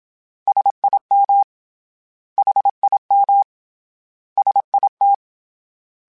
Morse code, heard as a keyed tone, about 27 wpm, 800 Hz. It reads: SIM HIM SIT
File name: S vs H / S vs H mp3s / S vs H @25